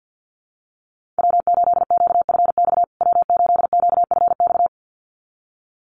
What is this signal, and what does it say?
Morse code, heard as a keyed tone, about 48 wpm, 700 Hz. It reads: W8QFX W8QFX
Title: USA Call Signs Repeated 2x 50wpm